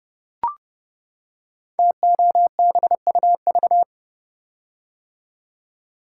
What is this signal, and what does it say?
Morse code, heard as a keyed tone, about 30 wpm, 700 Hz. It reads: TOBUV
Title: Code Groups - 5 Letters - 30wpm